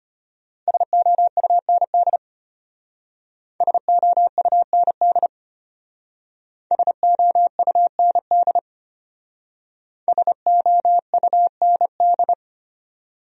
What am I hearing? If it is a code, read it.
Morse code, heard as a keyed tone, about 31 wpm, 700 Hz. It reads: SOUND SOUND SOUND SOUND
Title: Speed-Racing Top 300 Words 25wpm